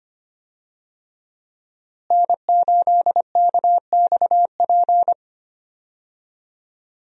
Morse code, heard as a keyed tone, about 25 wpm, 700 Hz. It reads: N8KXP
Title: USA Call Signs - 5 Characters - Repeated 3x - 25wpm